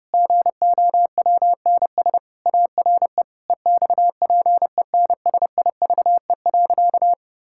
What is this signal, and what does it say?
Morse code, heard as a keyed tone, about 30 wpm, 700 Hz. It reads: GOWNS ARE EXPENSIVE.
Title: Pangram Challenge v2 - 30wpm